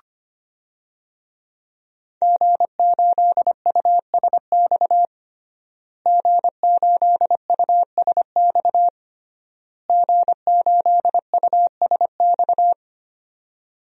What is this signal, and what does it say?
Morse code, heard as a keyed tone, about 25 wpm, 700 Hz. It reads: G8USX G8USX G8USX
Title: DXCC-100 Call Signs - 5 Characters - Repeated 3x - 25wpm